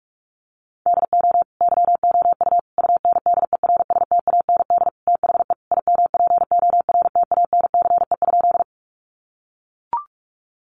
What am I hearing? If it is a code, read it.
Morse code, heard as a keyed tone, about 44 wpm, 700 Hz. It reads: DO YOU UNDERSTAND THE IMPORTANCE?
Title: Sentences from Top 1200 Words 45wpm